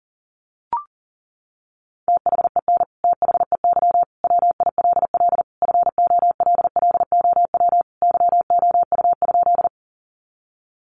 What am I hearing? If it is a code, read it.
Morse code, heard as a keyed tone, about 40 wpm, 700 Hz. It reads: THEN THEY WILL FOLLOW YOU?
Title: Sentences from Top 200 Words 40wpm